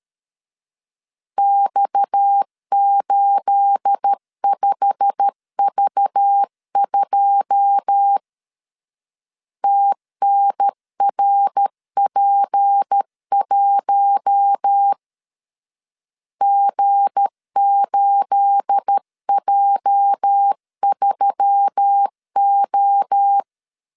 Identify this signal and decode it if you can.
Morse code, heard as a keyed tone, about 13 wpm, 800 Hz. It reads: X85V2 TNRP1 G8J3O
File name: LEZIONI / 9 28 12wpm 2351